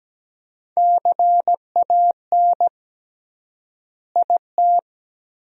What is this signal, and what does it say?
Morse code, heard as a keyed tone, about 17 wpm, 700 Hz. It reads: CAN IT
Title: 3x Word-Spacing - Sentences from Top 500 Words - 17wpm